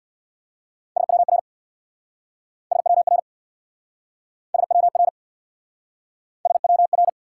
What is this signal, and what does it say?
Morse code, heard as a keyed tone, about 69 wpm, 700 Hz. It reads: SWR SWR SWR SWR